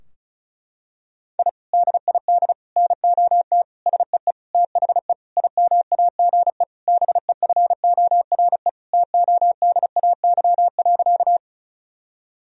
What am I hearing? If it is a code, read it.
Morse code, heard as a keyed tone, about 36 wpm, 700 Hz. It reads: I DID NOT SEE THE IMAGE BEFORE TODAY.
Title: Sentences from Top 1000 Words - Review Part 4 - 35wpm